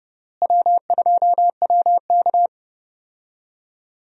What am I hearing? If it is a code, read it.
Morse code, heard as a keyed tone, about 30 wpm, 700 Hz. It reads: W2WK
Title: USA Call Signs - 4 Characters - Repeated 3x - 30wpm